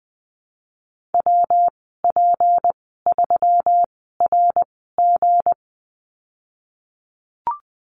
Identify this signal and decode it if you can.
Morse code, heard as a keyed tone, about 20 wpm, spaced as for 15 wpm, 700 Hz. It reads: WP3RG
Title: 200 Lesson 32g - Letter k - Call Signs 20wpm